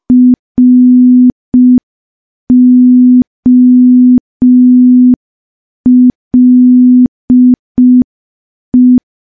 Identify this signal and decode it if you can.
Morse code, heard as a keyed tone, about 5 wpm, 250 Hz. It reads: ROLE
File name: morse / morse-training-05WPM